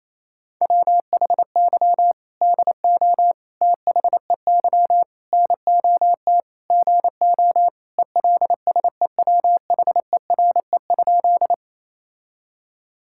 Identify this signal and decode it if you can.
Morse code, heard as a keyed tone, about 28 wpm, 700 Hz. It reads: WHY DO THEY NOT GO ELSEWHERE?
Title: Sentences from Top 1600 Words 28wpm